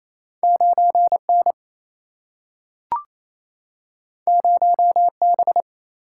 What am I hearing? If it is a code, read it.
Morse code, heard as a keyed tone, about 28 wpm, 700 Hz. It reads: 9 N 0 B
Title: Single-Letter Single-Number 28wpm